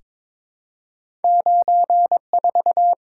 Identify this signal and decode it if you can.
Morse code, heard as a keyed tone, about 22 wpm, 700 Hz. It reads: 94
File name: practice / 22wpm / Two Letter-Number Combinations - 22wpm